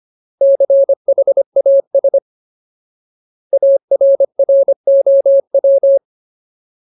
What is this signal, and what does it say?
Morse code, heard as a keyed tone, about 25 wpm, 550 Hz. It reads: CHAS ARROW
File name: Ultimate Word List / 25WPM Ultimate